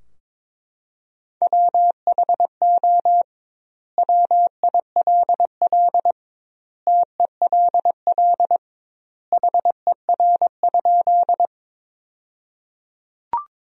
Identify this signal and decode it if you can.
Morse code, heard as a keyed tone, about 22 wpm, 700 Hz. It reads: WHO WILL TELL HER?
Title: 2x Word-Spacing - Sentences from Top 200 Words - 22wpm